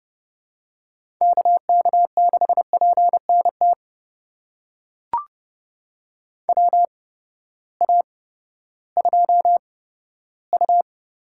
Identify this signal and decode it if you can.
Morse code, heard as a keyed tone, about 30 wpm, spaced as for 15 wpm, 700 Hz. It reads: KK6PNT WA2U
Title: USA Call Signs - 8x Character Spacing - 30wpm